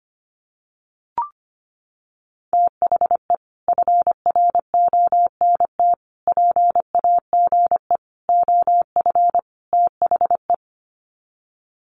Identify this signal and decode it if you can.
Morse code, heard as a keyed tone, about 25 wpm, 700 Hz. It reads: THE FRONT PAGE OF THE
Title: Sets of 5 Words Encore 25wpm